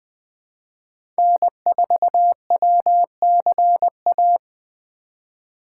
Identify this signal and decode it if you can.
Morse code, heard as a keyed tone, about 20 wpm, 700 Hz. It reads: N4WCA
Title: USA Call Signs - 5 Characters - 2x Character Spacing - 20wpm